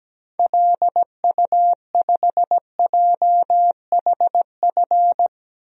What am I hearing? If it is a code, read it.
Morse code, heard as a keyed tone, about 17 wpm, 700 Hz. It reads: LU5JHF
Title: DXCC-100 Call Signs - 6 Characters - 17wpm